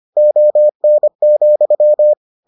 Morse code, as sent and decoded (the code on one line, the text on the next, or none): --- -. --..--
ON,